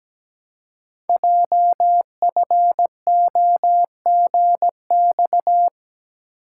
Morse code, heard as a keyed tone, about 17 wpm, 700 Hz. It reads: JFOGX
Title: Code Groups - 5 Letters - 17wpm